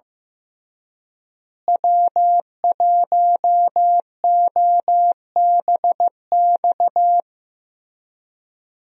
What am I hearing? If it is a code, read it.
Morse code, heard as a keyed tone, about 15 wpm, 700 Hz. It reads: W1OBX
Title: USA Call Signs - 5 Characters - Repeated 3x - 15wpm